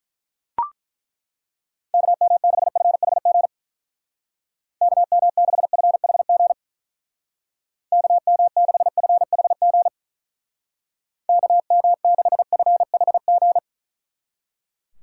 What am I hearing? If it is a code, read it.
Morse code, heard as a keyed tone, about 42 wpm, 700 Hz. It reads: KM6FHG KM6FHG KM6FHG KM6FHG
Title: Speed Racing - USA 2x3 Call Signs 35wpm